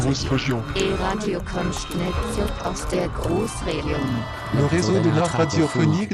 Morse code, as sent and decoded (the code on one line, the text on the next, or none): none